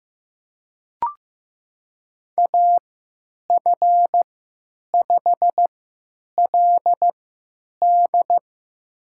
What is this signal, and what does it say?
Morse code, heard as a keyed tone, about 15 wpm, spaced as for 9 wpm, 700 Hz. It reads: AF5LD
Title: USA Call Signs - 5 Characters - 3x Character Spacing - 15wpm